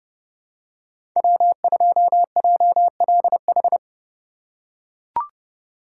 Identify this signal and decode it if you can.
Morse code, heard as a keyed tone, about 31 wpm, 700 Hz. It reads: W2JLH
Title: USA Call Signs - 5 Characters - 3x Character Spacing - 30wpm